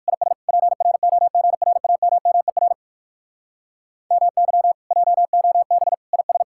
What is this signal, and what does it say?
Morse code, heard as a keyed tone, about 56 wpm, 700 Hz. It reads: IS PROGRAMMER MY JOB IS
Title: Speed-Racing Common QSO Phrases - 45wpm